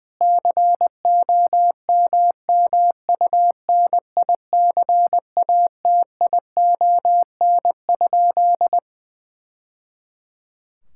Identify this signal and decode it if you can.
Morse code, heard as a keyed tone, about 20 wpm, 700 Hz. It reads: COMMUNICATION?
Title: Sentences from Top 1000 Words - Review Part 6 - 20wpm